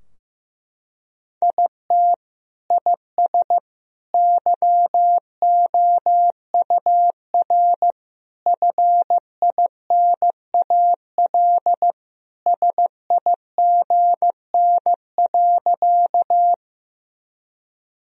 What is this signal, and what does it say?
Morse code, heard as a keyed tone, about 16 wpm, 700 Hz. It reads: IT IS YOUR FINAL SIGN.
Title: Sentences from Top 1000 Words - Review Part 2 - 15wpm